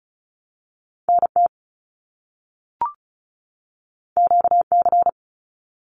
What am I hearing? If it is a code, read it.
Morse code, heard as a keyed tone, about 36 wpm, 700 Hz. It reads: NT QC
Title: Canadian Provinces and Territories 35wpm